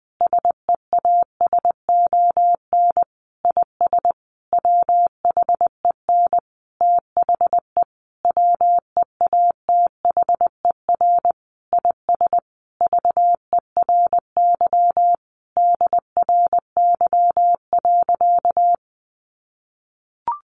Morse code, heard as a keyed tone, about 20 wpm, 700 Hz. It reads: SEASON IS WHEN THE WEATHER IS VERY DRY.
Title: Sentences from Top 1400 Words 20wpm